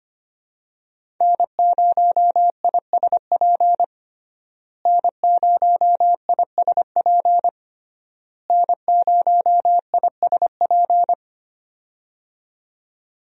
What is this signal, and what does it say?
Morse code, heard as a keyed tone, about 25 wpm, 700 Hz. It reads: N0ISP N0ISP N0ISP
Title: USA Call Signs - 5 Characters - Repeated 3x - 25wpm